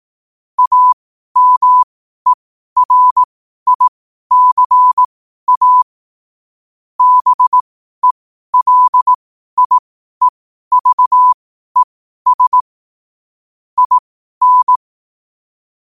Morse code, heard as a keyed tone, about 18 wpm, spaced as for 12 wpm, 1000 Hz. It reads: AMERICA BELIEVES IN
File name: qotdfiles / 12wpm